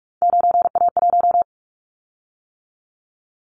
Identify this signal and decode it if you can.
Morse code, heard as a keyed tone, about 44 wpm, 700 Hz. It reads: 9A1
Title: Callsign Prefixes for Top 100 DXCC - 45wpm